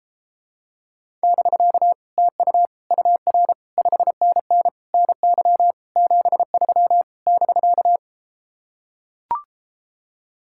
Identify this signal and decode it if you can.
Morse code, heard as a keyed tone, about 33 wpm, 700 Hz. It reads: <BK> TU UR 5NN NY 73 <BK>